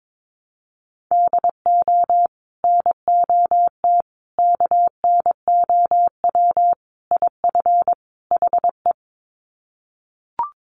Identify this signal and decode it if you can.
Morse code, heard as a keyed tone, about 22 wpm, 700 Hz. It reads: DO NOT KNOW IF HE
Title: Sets of 5 Words Encore2 22wpm